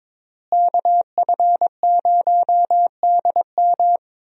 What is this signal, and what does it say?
Morse code, heard as a keyed tone, about 22 wpm, 700 Hz. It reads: KF0DM
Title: USA Call Signs - 5 Characters - Repeated 2x - 22wpm